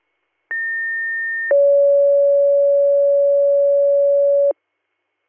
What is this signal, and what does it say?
Two-tone page, heard as a one-second tone, then a three-second tone, about 1800 then 560 Hz. It is heard from 0.5 to 4.5 s.